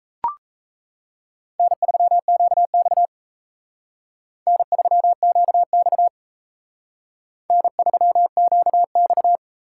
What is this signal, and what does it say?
Morse code, heard as a keyed tone, about 34 wpm, 700 Hz. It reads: N3QX N3QX N3QX
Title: Speed Racing - USA Call Signs 28wpm